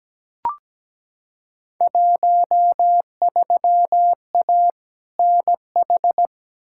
Morse code, heard as a keyed tone, about 17 wpm, 700 Hz. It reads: 13A NH
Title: Field Day Exchanges Encore - 17wpm